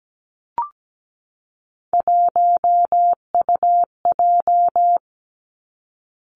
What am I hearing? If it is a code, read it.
Morse code, heard as a keyed tone, about 17 wpm, 700 Hz. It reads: 1UJ
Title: Three Challenging Letter-Number Combinations 17wpm